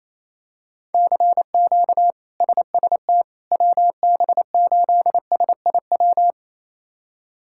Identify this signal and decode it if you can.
Morse code, heard as a keyed tone, about 28 wpm, 700 Hz. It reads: CQ SST WB8SIW
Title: K1USN SST QSO Practice - 28wpm